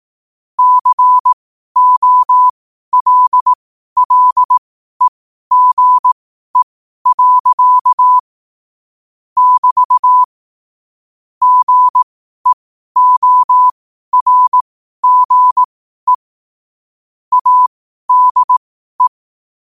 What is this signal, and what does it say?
Morse code, heard as a keyed tone, about 18 wpm, spaced as for 12 wpm, 1000 Hz. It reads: COLLEGE. <BT> GEORGE ADE
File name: qotdfiles / 12wpm